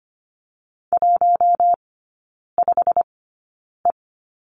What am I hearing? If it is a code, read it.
Morse code, heard as a keyed tone, about 25 wpm, spaced as for 9 wpm, 700 Hz. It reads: 15E
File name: Lesson 15e - Number 5 - Three Characters 25wpm